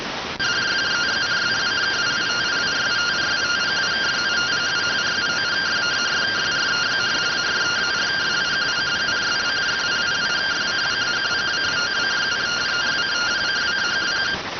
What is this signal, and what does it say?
Two-tone teleprinter signal, about 170 Hz shift.